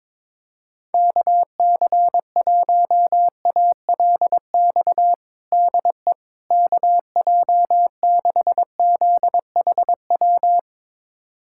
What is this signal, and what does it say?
Morse code, heard as a keyed tone, about 22 wpm, 700 Hz. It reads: KC1ALX DE KJ6ZHW
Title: USA Callsign DE Callsign - 6 Characters - 22wpm